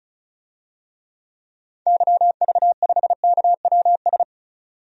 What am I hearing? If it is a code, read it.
Morse code, heard as a keyed tone, about 36 wpm, 700 Hz. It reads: YV5KWS
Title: DXCC-100 Call Signs - 6 Characters - 35wpm